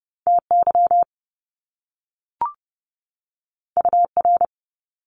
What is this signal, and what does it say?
Morse code, heard as a keyed tone, about 30 wpm, 700 Hz. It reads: TY UR
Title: Top Two Letter Combinations in Words 30wpm